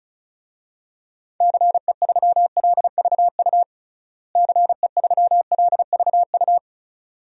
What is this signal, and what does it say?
Morse code, heard as a keyed tone, about 36 wpm, 700 Hz. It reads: CE3LVU CE3LVU
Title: DXCC-100 Call Signs - 6 Characters - Repeated 2x - 35wpm